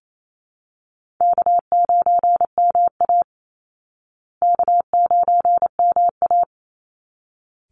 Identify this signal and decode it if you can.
Morse code, heard as a keyed tone, about 28 wpm, 700 Hz. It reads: K9MA K9MA
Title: Call Signs Repeated 3x - 28wpm